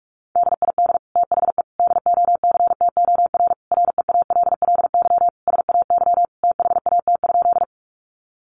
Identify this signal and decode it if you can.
Morse code, heard as a keyed tone, about 44 wpm, 700 Hz. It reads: DID THE DOCTOR REALLY SAY THAT?
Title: Sentences from Top 1500 Words 45wpm